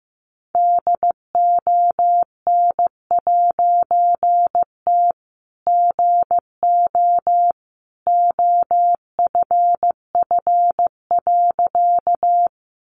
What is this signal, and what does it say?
Morse code, heard as a keyed tone, about 15 wpm, 700 Hz. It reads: DON'T GO OFF.
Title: A Sound Of Thunder 15wpm